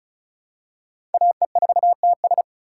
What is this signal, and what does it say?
Morse code, heard as a keyed tone, about 34 wpm, 700 Hz. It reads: AE4TS
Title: USA Call Signs - 3x Character Spacing - 35wpm